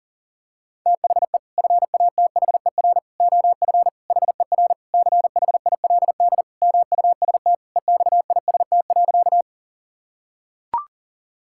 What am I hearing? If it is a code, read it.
Morse code, heard as a keyed tone, about 37 wpm, 700 Hz. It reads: THE FATHER OF HER CHILD MUST EXIST.
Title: Sentences from Top 1000 Words - Review Part 5 - 40wpm